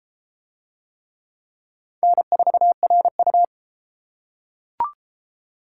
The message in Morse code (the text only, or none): N4RU